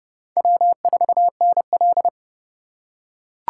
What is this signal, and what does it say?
Morse code, heard as a keyed tone, about 30 wpm, 700 Hz. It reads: W4NL
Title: USA Call Signs - 4 Characters - 2x Character Spacing - 30wpm